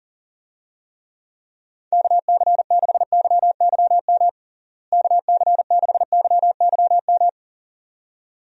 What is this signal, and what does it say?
Morse code, heard as a keyed tone, about 40 wpm, 700 Hz. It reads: KC6YYM KC6YYM
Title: USA Call Signs - 6 Characters - Repeated 2x - 40wpm